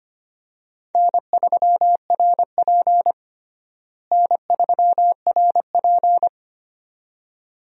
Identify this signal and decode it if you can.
Morse code, heard as a keyed tone, about 25 wpm, 700 Hz. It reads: N3RP N3RP
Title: USA Call Signs - 4 Characters - Repeated 2x - 25wpm